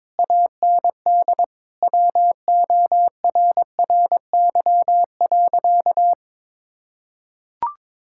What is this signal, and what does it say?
Morse code, heard as a keyed tone, about 22 wpm, 700 Hz. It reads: AND WORRY.
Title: Pangram Challenge v2 - 22wpm